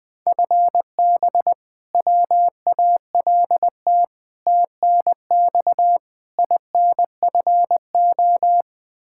Morse code, heard as a keyed tone, about 21 wpm, 700 Hz. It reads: FB WALT TNX INFO
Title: Practice QSO - Exchange 3 - More Info - 20wpm